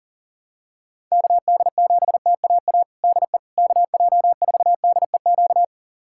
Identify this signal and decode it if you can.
Morse code, heard as a keyed tone, about 40 wpm, 700 Hz. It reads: KD7TAA DE KJ4DEQ
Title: USA Callsign DE Callsign 40wpm